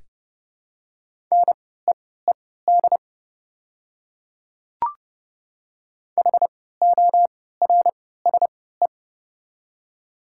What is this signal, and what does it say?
Morse code, heard as a keyed tone, about 30 wpm, spaced as for 17 wpm, 700 Hz. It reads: NEED HORSE
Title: 3x Character Spacing - Top 500 Words - 30wpm